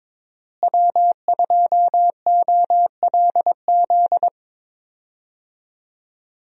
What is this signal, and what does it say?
Morse code, heard as a keyed tone, about 22 wpm, 700 Hz. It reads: W2OLZ
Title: USA Call Signs - 5 Characters - Repeated 3x - 22wpm